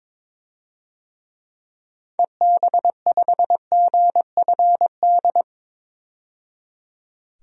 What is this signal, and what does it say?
Morse code, heard as a keyed tone, about 22 wpm, 700 Hz. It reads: EB5GFD
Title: DXCC-100 Call Signs - 6 Characters - 22wpm